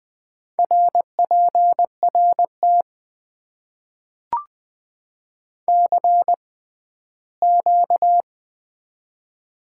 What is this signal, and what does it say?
Morse code, heard as a keyed tone, about 21 wpm, spaced as for 14 wpm, 700 Hz. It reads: RPRT CQ
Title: Top 90 QSO Elements - 6x Character Spacing - 20wpm